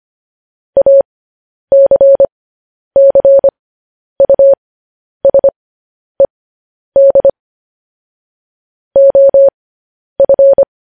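Morse code, heard as a keyed tone, about 25 wpm, spaced as for 10 wpm, 550 Hz. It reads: ACCUSED OF
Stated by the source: News headlines sent in Morse at 10 wpm.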